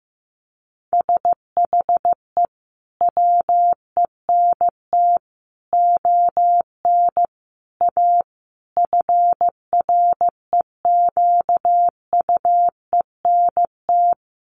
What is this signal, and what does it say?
Morse code, heard as a keyed tone, about 15 wpm, 700 Hz. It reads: SHE WENT ON A FREQUENT
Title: Sentences from Top 1400 Words 15wpm